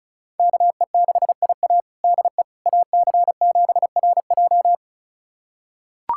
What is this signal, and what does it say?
Morse code, heard as a keyed tone, about 36 wpm, 700 Hz. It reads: KE6IA DE AC7RJ
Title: USA Callsign DE Callsign - 5 Characters - 35wpm